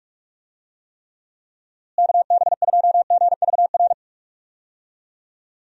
Morse code, heard as a keyed tone, about 46 wpm, 700 Hz. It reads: KB2GVR
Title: USA Call Signs - 6 Characters - Repeated 3x - 45wpm